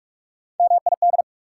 Morse code, heard as a keyed tone, about 47 wpm, 700 Hz. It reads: MID
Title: Prefixes and Suffixes 45wpm